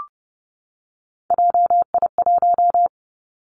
Morse code, heard as a keyed tone, about 30 wpm, 700 Hz. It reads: JI1
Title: Three Challenging Letter-Number Combinations 30wpm